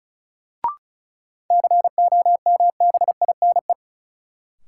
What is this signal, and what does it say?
Morse code, heard as a keyed tone, about 36 wpm, 700 Hz. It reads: COMBINE
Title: Sentences from Top 1600 Words - New Word Drill - 35wpm